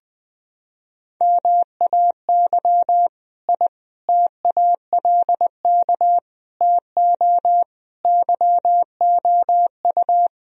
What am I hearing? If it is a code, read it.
Morse code, heard as a keyed tone, about 21 wpm, 700 Hz. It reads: MAY I TALK TO YOU
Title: Sentences from Top 1000 Words - Review Part 4 - 20wpm